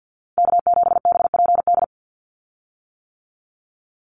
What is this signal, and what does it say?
Morse code, heard as a keyed tone, about 50 wpm, 700 Hz. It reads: K7BPD
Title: USA Call Signs 50wpm